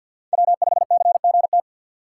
Morse code, heard as a keyed tone, about 49 wpm, 700 Hz. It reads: W5CGT
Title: USA Call Signs - 5 Characters - 3x Character Spacing - 50wpm